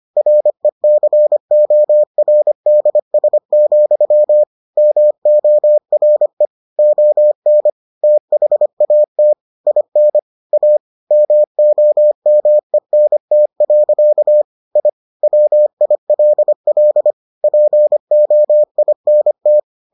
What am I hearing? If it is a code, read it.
Morse code, heard as a keyed tone, about 25 wpm, 600 Hz. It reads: RECORDS, MORE ON THAT IN A MOMENT. I WILL POINT